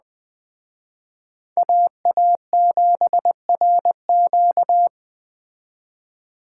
Morse code, heard as a keyed tone, about 20 wpm, 700 Hz. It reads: AA7RQ
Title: Speed Racing - USA 2x2 Call Signs 17wpm